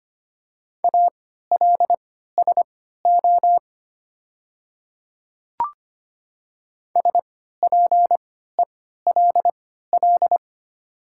Morse code, heard as a keyed tone, about 25 wpm, spaced as for 14 wpm, 700 Hz. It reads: ALSO SPELL
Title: 3x Character Spacing - Top 200 Words - 25wpm